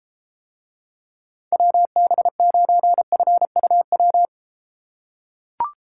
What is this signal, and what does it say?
Morse code, heard as a keyed tone, about 33 wpm, 700 Hz. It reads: WB9FUW